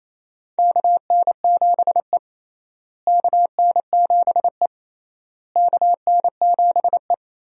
Morse code, heard as a keyed tone, about 28 wpm, 700 Hz. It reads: KN7E KN7E KN7E
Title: USA Call Signs - 4 Characters - Repeated 3x - 28wpm